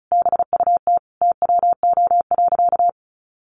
Morse code, heard as a keyed tone, about 34 wpm, 700 Hz. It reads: BUT TWO.
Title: Sentences from Top 1400 Words 35wpm